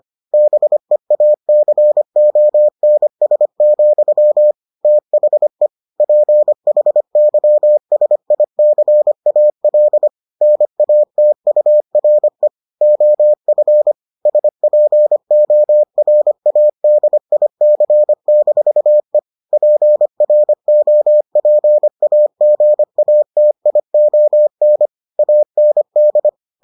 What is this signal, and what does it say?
Morse code, heard as a keyed tone, about 25 wpm, 600 Hz. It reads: BEACONS, THE PHYSICAL NATURE OF SPORADIC-E PROPAGATION AND